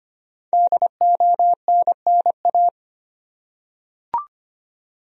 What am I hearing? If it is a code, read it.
Morse code, heard as a keyed tone, about 25 wpm, 700 Hz. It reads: DONNA
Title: USA Common Names - 25wpm